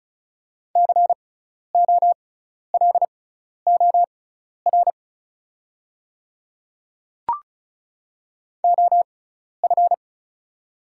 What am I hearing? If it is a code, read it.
Morse code, heard as a keyed tone, about 36 wpm, spaced as for 12 wpm, 690 Hz. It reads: COLOR OF
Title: 6x Character Spacing - Top 300 Words - 35wpm